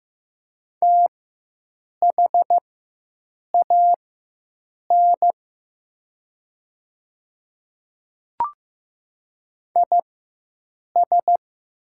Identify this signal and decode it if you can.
Morse code, heard as a keyed tone, about 15 wpm, spaced as for 7 wpm, 700 Hz. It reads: THAN IS
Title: 4x Character Spacing - Top 100 Words - 15wpm